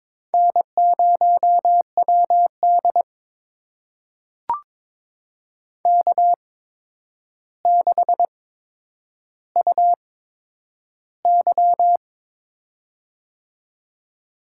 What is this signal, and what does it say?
Morse code, heard as a keyed tone, about 22 wpm, spaced as for 9 wpm, 700 Hz. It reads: N0WD K6UY
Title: USA Call Signs - 4 Characters - 8x Character Spacing - 22wpm